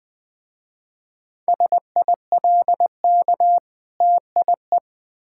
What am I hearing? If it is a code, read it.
Morse code, heard as a keyed tone, about 21 wpm, 700 Hz. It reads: SILK TIE
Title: Difficult Words with Hints - 20wpm